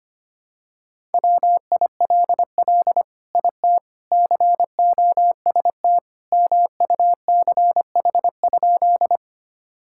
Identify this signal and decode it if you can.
Morse code, heard as a keyed tone, about 25 wpm, 700 Hz. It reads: WILL IT COST MUCH?
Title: Sentences from Top 1000 Words - Review Part 5 - 25wpm